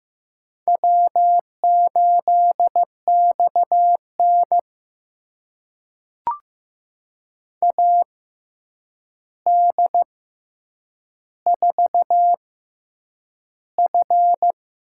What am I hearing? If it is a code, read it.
Morse code, heard as a keyed tone, about 15 wpm, spaced as for 8 wpm, 700 Hz. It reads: W8XN AD4F
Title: USA Call Signs - 4 Characters - 6x Character Spacing - 15wpm